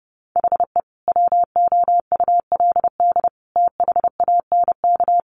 Morse code, heard as a keyed tone, about 30 wpm, 700 Hz. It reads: HE WOULD THANK
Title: Sentences from Top 1100 Words 30wpm